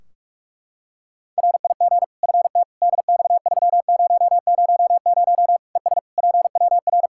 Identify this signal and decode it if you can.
Morse code, heard as a keyed tone, about 46 wpm, 700 Hz. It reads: RIG FT DX3000 ES PWR
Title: Practice QSO - Exchange 2 - Info - 45wpm